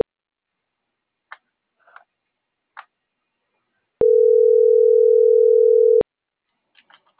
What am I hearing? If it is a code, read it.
Ringback tone (North America), 440 + 480 Hz.
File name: CENTRAL MABE / out saby.reyes25